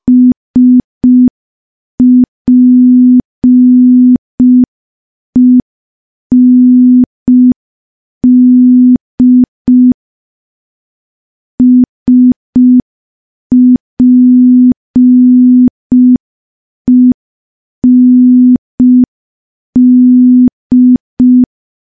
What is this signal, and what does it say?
Morse code, heard as a keyed tone, about 5 wpm, 250 Hz. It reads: SPEND SPEND
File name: morse / morse-training-05WPM